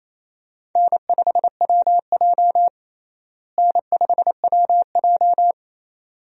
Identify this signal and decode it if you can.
Morse code, heard as a keyed tone, about 28 wpm, 700 Hz. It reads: N5WJ N5WJ
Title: USA Call Signs - 4 Characters - Repeated 2x - 28wpm